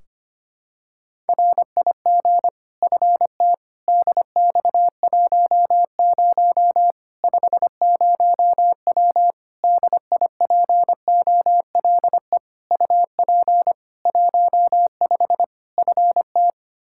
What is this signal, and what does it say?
Morse code, heard as a keyed tone, about 25 wpm, 700 Hz. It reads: RIG FT DX10 50W DIPOLE UP 15 FT
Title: K1USN SST QSO Practice 25wpm